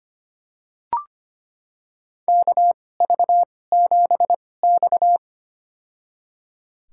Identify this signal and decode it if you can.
Morse code, heard as a keyed tone, about 25 wpm, spaced as for 18 wpm, 700 Hz. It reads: KV7X
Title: USA Call Signs - 4 Characters - 2x Character Spacing - 25wpm